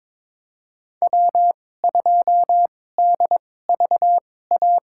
Morse code, heard as a keyed tone, about 23 wpm, spaced as for 16 wpm, 690 Hz. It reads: W2DVA
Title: USA Call Signs - 5 Characters - 2x Character Spacing - 22wpm